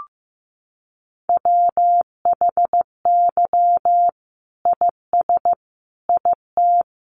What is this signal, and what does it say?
Morse code, heard as a keyed tone, about 15 wpm, 700 Hz. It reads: WHY IS IT
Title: Sentences from Top 900 Words 15wpm